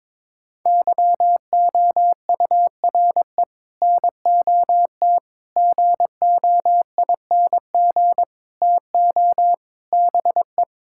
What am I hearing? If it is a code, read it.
Morse code, heard as a keyed tone, about 22 wpm, 700 Hz. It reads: YOURE NOT GOING TO BE
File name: practice / 22wpm / Sets of 7 Words - 22wpm